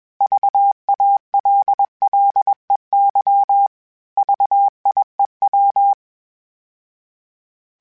Morse code, heard as a keyed tone, about 21 wpm, 800 Hz. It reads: VALLEY VIEW
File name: fromSMs / FromSMs mp3s / Cities@20